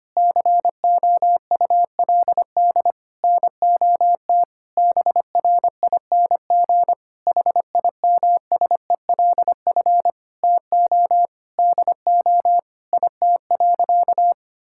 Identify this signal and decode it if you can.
Morse code, heard as a keyed tone, about 25 wpm, 700 Hz. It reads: COULD NOT BRING HIMSELF TO DO IT.
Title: Sentences from Top 1600 Words 25wpm